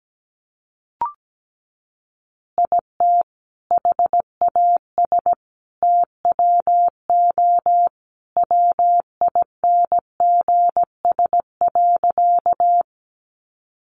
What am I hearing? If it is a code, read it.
Morse code, heard as a keyed tone, about 17 wpm, 700 Hz. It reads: IT HAS TWO WINGS.
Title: Additional Easy Phrases 17wpm